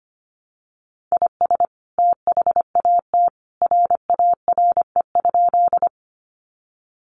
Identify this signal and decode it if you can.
Morse code, heard as a keyed tone, about 25 wpm, 700 Hz. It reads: IS THAT RARE?
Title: Sentences from Top 1100 Words 25wpm